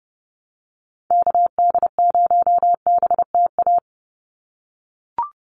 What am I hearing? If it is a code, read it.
Morse code, heard as a keyed tone, about 30 wpm, 700 Hz. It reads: KD0BTA
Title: USA Call Signs 30wpm